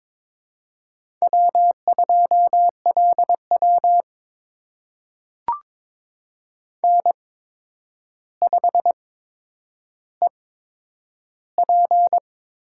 Morse code, heard as a keyed tone, about 22 wpm, spaced as for 9 wpm, 700 Hz. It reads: W2LW N5EP
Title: USA Call Signs - 4 Characters - 8x Character Spacing - 22wpm